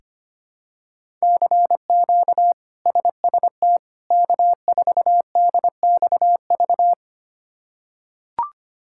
Morse code, heard as a keyed tone, about 25 wpm, 700 Hz. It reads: CQ SST K4DXV
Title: K1USN SST QSO Practice - 25wpm